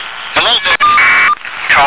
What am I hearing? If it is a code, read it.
Caller ID data burst, with speech.